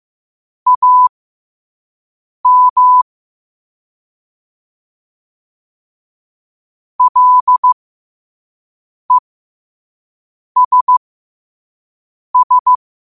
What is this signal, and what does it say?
Morse code, heard as a keyed tone, about 15 wpm, spaced as for 5 wpm, 1000 Hz. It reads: AM LESS
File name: qotdfiles / 5wpm